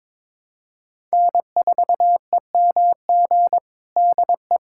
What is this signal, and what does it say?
Morse code, heard as a keyed tone, about 21 wpm, 700 Hz. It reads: N4EMG DE
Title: USA Callsign DE Callsign - 5 Characters - 22wpm